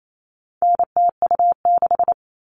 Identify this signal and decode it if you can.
Morse code, heard as a keyed tone, about 28 wpm, 700 Hz. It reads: NTU6
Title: All Cut Numbers - 4-Digit 28wpm